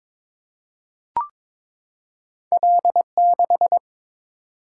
Morse code, heard as a keyed tone, about 22 wpm, 700 Hz. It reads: L6
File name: practice / 22wpm / Two Letter-Number Combinations - 22wpm